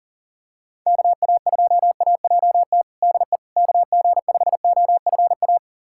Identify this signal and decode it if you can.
Morse code, heard as a keyed tone, about 40 wpm, 700 Hz. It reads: KA2AJT DE KG5OFA
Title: USA Callsign DE Callsign - 6 Characters - 40wpm